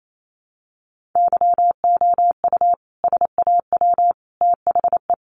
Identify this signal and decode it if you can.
Morse code, heard as a keyed tone, about 28 wpm, 700 Hz. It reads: YOU SAW THE
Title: Sentences from Top 400 Words 28wpm